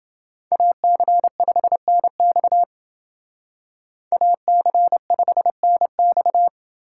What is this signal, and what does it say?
Morse code, heard as a keyed tone, about 29 wpm, 700 Hz. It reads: AC5NX AC5NX